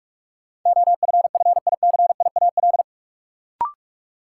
Morse code, heard as a keyed tone, about 42 wpm, 700 Hz. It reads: OFFICIAL
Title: Sentences from Top 1000 Words - New Word Drill - 45wpm